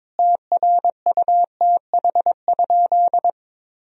Morse code, heard as a keyed tone, about 22 wpm, 700 Hz. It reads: TRUTH?
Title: Sentences from Top 1000 Words - Review Part 2 - 22wpm